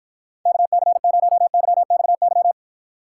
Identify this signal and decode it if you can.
Morse code, heard as a keyed tone, about 53 wpm, 700 Hz. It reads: KC0YXY